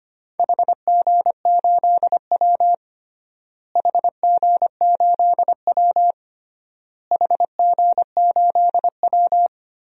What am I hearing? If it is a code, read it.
Morse code, heard as a keyed tone, about 25 wpm, 700 Hz. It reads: HG8W HG8W HG8W
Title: DXCC-100 Call Signs - 4 Characters - Repeated 3x - 25wpm